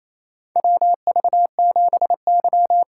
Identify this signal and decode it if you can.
Morse code, heard as a keyed tone, about 28 wpm, 700 Hz. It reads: WV7Y
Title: USA Call Signs - 3x Character Spacing - 28wpm